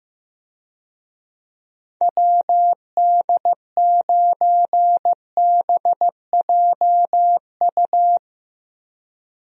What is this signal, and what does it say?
Morse code, heard as a keyed tone, about 15 wpm, 700 Hz. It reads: WD9BJU
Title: USA Call Signs - 6 Characters - 15wpm